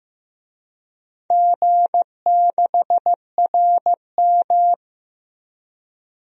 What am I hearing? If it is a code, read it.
Morse code, heard as a keyed tone, about 15 wpm, 700 Hz. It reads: G6RM
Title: DXCC-100 Call Signs - 4 Characters - 15wpm